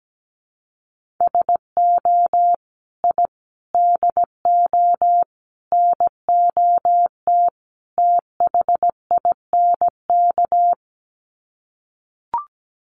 Morse code, heard as a keyed tone, about 17 wpm, 700 Hz. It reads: SO I DO NOT THINK
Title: Sets of 5 Words 17wpm